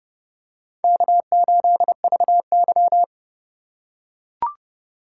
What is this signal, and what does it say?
Morse code, heard as a keyed tone, about 30 wpm, 700 Hz. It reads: K8VY
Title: USA Call Signs - 4 Characters - 3x Character Spacing - 30wpm